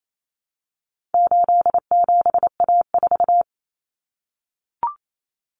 Morse code, heard as a keyed tone, about 28 wpm, 700 Hz. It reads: 87A4
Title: Common Cut Numbers - 4-Digit 28wpm